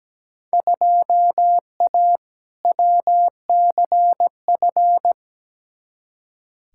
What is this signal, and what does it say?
Morse code, heard as a keyed tone, about 17 wpm, 700 Hz. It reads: 2A WCF
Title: Field Day Exchanges Encore - 17wpm